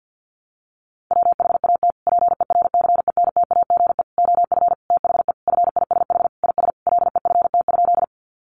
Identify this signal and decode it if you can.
Morse code, heard as a keyed tone, about 50 wpm, 700 Hz. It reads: WHAT PERCENTAGE OF THE FISH IS LEFT?
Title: Sentences from Top 1300 Words 50wpm